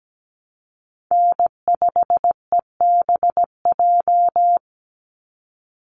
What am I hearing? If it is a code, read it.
Morse code, heard as a keyed tone, about 17 wpm, 700 Hz. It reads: N5EBJ
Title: USA Call Signs 17wpm